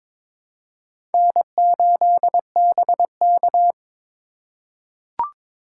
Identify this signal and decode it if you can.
Morse code, heard as a keyed tone, about 22 wpm, 700 Hz. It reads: N8BK
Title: USA Call Signs - 4 Characters - 3x Character Spacing - 22wpm